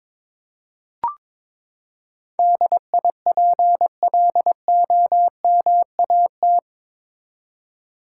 Morse code, heard as a keyed tone, about 22 wpm, 700 Hz. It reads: DIPLOMAT
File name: practice / 22wpm / Eight Letter Words - 22wpm